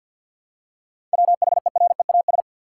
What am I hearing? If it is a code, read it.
Morse code, heard as a keyed tone, about 50 wpm, 700 Hz. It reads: WHEREAS
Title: Seven Letter Words - 50wpm